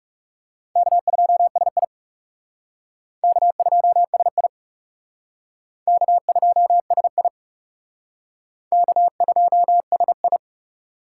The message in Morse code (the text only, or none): K2SI K2SI K2SI K2SI